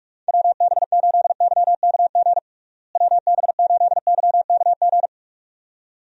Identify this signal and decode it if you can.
Morse code, heard as a keyed tone, about 46 wpm, 700 Hz. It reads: WB8YKG WB8YKG
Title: USA Call Signs - 6 Characters - Repeated 2x - 45wpm